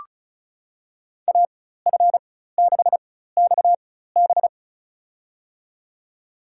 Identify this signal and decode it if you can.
Morse code, heard as a keyed tone, about 34 wpm, spaced as for 16 wpm, 700 Hz. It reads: AF6XB
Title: USA Call Signs - 5 Characters - 4x Character Spacing - 35wpm